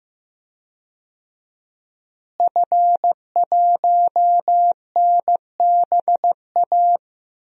Morse code, heard as a keyed tone, about 15 wpm, 700 Hz. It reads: F1NBA
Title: DXCC-100 Call Signs - 5 Characters - Repeated 2x - 15wpm